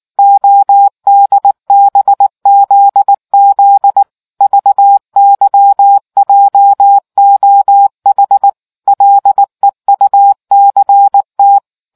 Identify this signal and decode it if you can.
Morse code, heard as a keyed tone, about 19 wpm, 800 Hz. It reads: ODBZZ VYJOH LEUCT